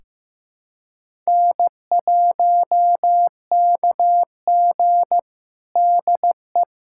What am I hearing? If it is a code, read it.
Morse code, heard as a keyed tone, about 16 wpm, 700 Hz. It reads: N1KG DE
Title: USA Callsign DE Callsign - 4 Characters - 15wpm